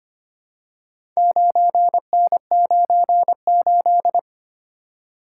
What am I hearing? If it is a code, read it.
Morse code, heard as a keyed tone, about 25 wpm, 690 Hz. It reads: 9N98
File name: practice / 25wpm / Four Letter-Number Combinations - 25wpm